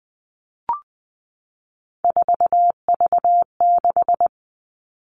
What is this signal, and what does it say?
Morse code, heard as a keyed tone, about 20 wpm, 700 Hz. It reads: 4V6
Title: All Cut Numbers - 3-Digit 20wpm